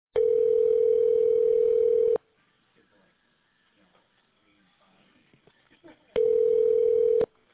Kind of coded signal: telephone call progress tone